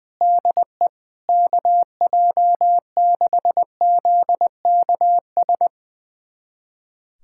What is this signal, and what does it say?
Morse code, heard as a keyed tone, about 20 wpm, 700 Hz. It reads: DE KJ6ZKS
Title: USA Callsign DE Callsign - 6 Characters - 20wpm